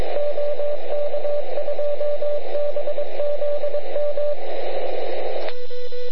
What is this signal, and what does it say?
Morse code, heard as a keyed tone, about 22 wpm, 580 Hz, with radio noise. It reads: OK2BZM
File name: ARCHIV / QRP2005 / ok2bzm